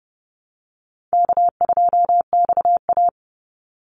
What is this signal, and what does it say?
Morse code, heard as a keyed tone, about 29 wpm, 700 Hz. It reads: K2XA
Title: Easy Call Signs Repeated 3x - 30wpm